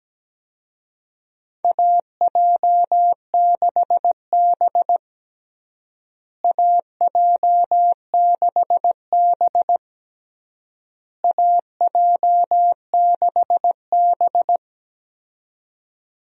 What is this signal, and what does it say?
Morse code, heard as a keyed tone, about 17 wpm, 700 Hz. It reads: AJ6B AJ6B AJ6B
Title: USA Call Signs - 4 Characters - Repeated 3x - 17wpm